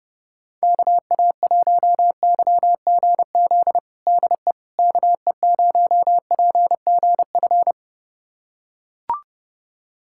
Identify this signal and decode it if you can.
Morse code, heard as a keyed tone, about 30 wpm, 700 Hz. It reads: KA1YGZ DE KE0PGF
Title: USA Callsign DE Callsign - 6 Characters - 30wpm